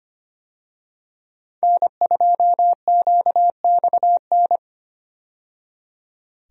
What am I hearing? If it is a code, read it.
Morse code, heard as a keyed tone, about 25 wpm, 700 Hz. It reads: N2QXN
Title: USA Call Signs - 5 Characters - 25wpm